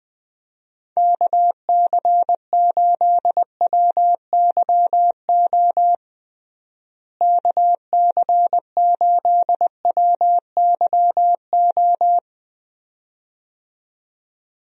Morse code, heard as a keyed tone, about 21 wpm, 700 Hz. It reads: KC8WYO KC8WYO
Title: USA Call Signs - 6 Characters - Repeated 2x - 20wpm